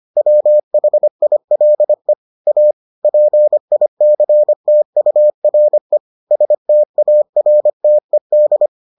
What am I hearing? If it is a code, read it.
Morse code, heard as a keyed tone, about 25 wpm, 600 Hz. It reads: WHILE A PICTURE STARTED